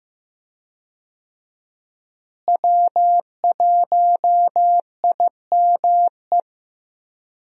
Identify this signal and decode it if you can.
Morse code, heard as a keyed tone, about 15 wpm, 700 Hz. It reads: W1IME
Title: USA Call Signs - 5 Characters - Repeated 2x - 15wpm